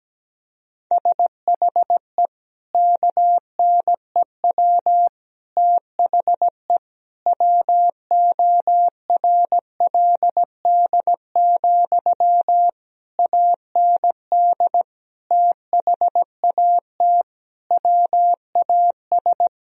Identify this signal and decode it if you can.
Morse code, heard as a keyed tone, about 17 wpm, 700 Hz. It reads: SHE KNEW THE WORLD, AND THAT WAS
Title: Sentences from Top 1000 Words - Review Part 4 - 17wpm